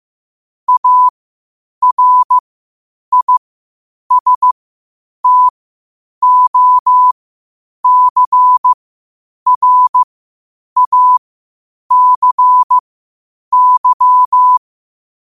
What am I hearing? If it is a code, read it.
Morse code, heard as a keyed tone, about 15 wpm, spaced as for 8 wpm, 1000 Hz. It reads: ARISTOCRACY
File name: qotdfiles / 8wpm